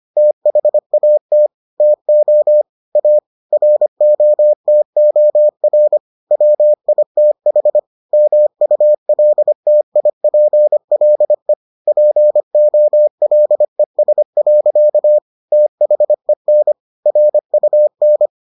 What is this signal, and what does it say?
Morse code, heard as a keyed tone, about 25 wpm, 600 Hz. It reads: THAT TO A ROTOR WITH MULTIPLE POLES. THEN RUN